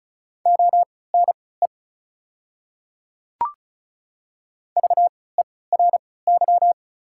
Morse code, heard as a keyed tone, about 36 wpm, spaced as for 20 wpm, 690 Hz. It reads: ONE VERY
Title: 3x Character Spacing - Top 200 Words - 35wpm